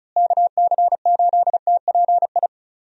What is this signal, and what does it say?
Morse code, heard as a keyed tone, about 36 wpm, 700 Hz. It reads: KC8TPI
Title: USA Call Signs - 6 Characters - Repeated 2x - 35wpm